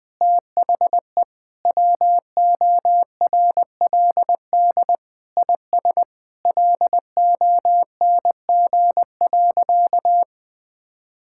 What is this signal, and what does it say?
Morse code, heard as a keyed tone, about 20 wpm, 700 Hz. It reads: THE WORLD IS LONG.
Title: Sentences with 3-Word Phrases - 20wpm